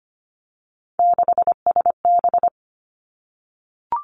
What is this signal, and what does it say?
Morse code, heard as a keyed tone, about 25 wpm, 700 Hz. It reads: 6SB
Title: Three Challenging Letter-Number Combinations 25wpm